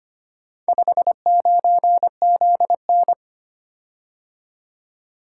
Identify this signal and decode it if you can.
Morse code, heard as a keyed tone, about 25 wpm, 700 Hz. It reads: 59ZN
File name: practice / 25wpm / Four Letter-Number Combinations - 25wpm